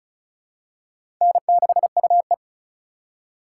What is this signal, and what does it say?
Morse code, heard as a keyed tone, about 34 wpm, 700 Hz. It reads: N6UE
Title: USA Call Signs - 4x Character Spacing - 35wpm